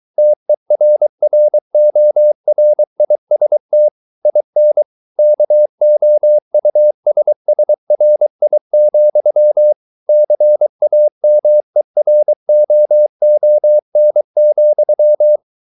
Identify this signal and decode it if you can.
Morse code, heard as a keyed tone, about 23 wpm, 600 Hz. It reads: TERRORIST IN KOUSSRI, CAMEROON,